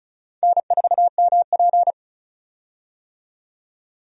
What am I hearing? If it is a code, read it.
Morse code, heard as a keyed tone, about 34 wpm, 700 Hz. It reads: N4MP
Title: USA Call Signs - 4 Characters - 35wpm